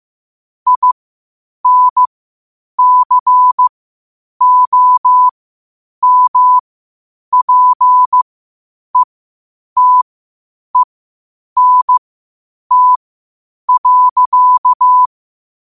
Morse code, heard as a keyed tone, about 15 wpm, spaced as for 8 wpm, 1000 Hz. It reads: INCOMPETENT.
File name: qotdfiles / 8wpm